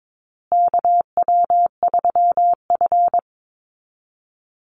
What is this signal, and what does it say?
Morse code, heard as a keyed tone, about 22 wpm, 700 Hz. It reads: KW3F
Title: Call Signs Repeated 2x - 22wpm